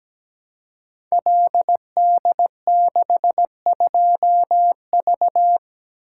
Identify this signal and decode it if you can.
Morse code, heard as a keyed tone, about 17 wpm, 700 Hz. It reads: LD62V
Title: Code Groups - 5 Letter:Numbers - 17wpm